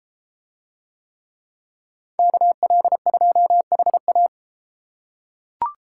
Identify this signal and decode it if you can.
Morse code, heard as a keyed tone, about 31 wpm, 700 Hz. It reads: KL2HA